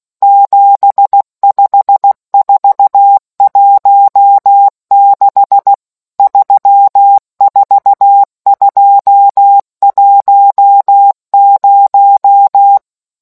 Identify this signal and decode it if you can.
Morse code, heard as a keyed tone, about 16 wpm, 800 Hz. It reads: 75416 34210